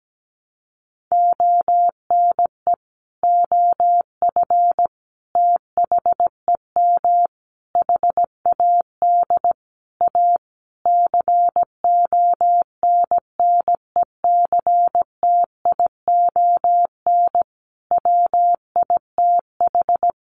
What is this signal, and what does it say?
Morse code, heard as a keyed tone, about 17 wpm, 700 Hz. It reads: ONE OF THEM HAD A CONNECTION WITH
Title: Sentences from Top 1300 Words 17wpm